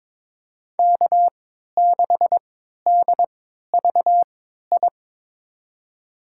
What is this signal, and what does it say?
Morse code, heard as a keyed tone, about 23 wpm, spaced as for 13 wpm, 700 Hz. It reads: K6DVI
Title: USA Call Signs - 3x Character Spacing - 22wpm